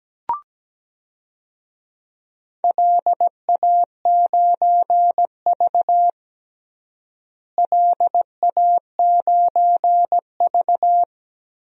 Morse code, heard as a keyed tone, about 17 wpm, 700 Hz. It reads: LA9V LA9V
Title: DXCC-100 Call Signs - 4 Characters - Repeated 3x - 17wpm